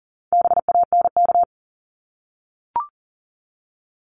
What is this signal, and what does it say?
Morse code, heard as a keyed tone, about 40 wpm, 700 Hz. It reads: BANK
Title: Top 751-1000 Words 40wpm